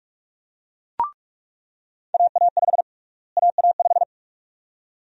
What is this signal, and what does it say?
Morse code, heard as a keyed tone, about 46 wpm, 700 Hz. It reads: AA5 AA5
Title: USA Call Sign Prefixes - Repeated 2x - 45wpm